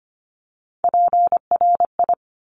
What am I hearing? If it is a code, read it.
Morse code, heard as a keyed tone, about 25 wpm, 700 Hz. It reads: PRI
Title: Top Three Letter Combinations in Words 25wpm